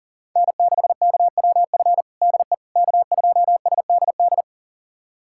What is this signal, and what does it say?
Morse code, heard as a keyed tone, about 40 wpm, 690 Hz. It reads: N6KWF DE K2SDD
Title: USA Callsign DE Callsign - 5 Characters - 40wpm